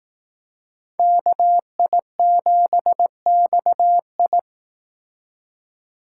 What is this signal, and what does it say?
Morse code, heard as a keyed tone, about 18 wpm, 700 Hz. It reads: KI7XI
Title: Speed Racing - USA 2x2 Call Signs 15wpm